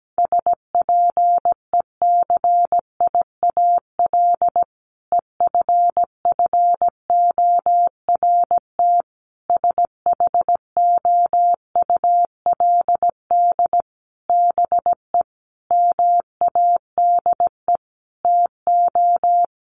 Morse code, heard as a keyed tone, about 17 wpm, 700 Hz. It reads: SPECIAL EFFORT SHOULD BE MADE TO
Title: Sentences from Top 1200 Words 17wpm